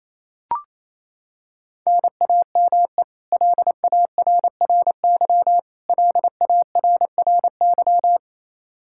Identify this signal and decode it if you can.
Morse code, heard as a keyed tone, about 27 wpm, 700 Hz. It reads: NAME LARRY LARRY
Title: Practice QSO - Exchange 1 - Beginning - 28wpm